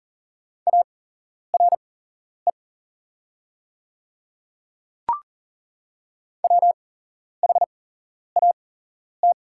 Morse code, heard as a keyed tone, about 40 wpm, spaced as for 11 wpm, 700 Hz. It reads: ARE WHAT
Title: 8x Character Spacing - Top 200 Words - 40wpm